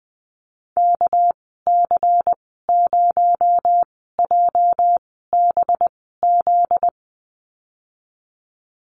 Morse code, heard as a keyed tone, about 20 wpm, spaced as for 14 wpm, 700 Hz. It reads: KC0JBZ
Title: 245 Lesson 39g - Letter z - Call Signs 20wpm